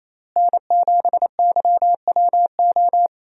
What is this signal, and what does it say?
Morse code, heard as a keyed tone, about 28 wpm, 700 Hz. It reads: N7YWO
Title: USA Call Signs - 5 Characters - Repeated 3x - 28wpm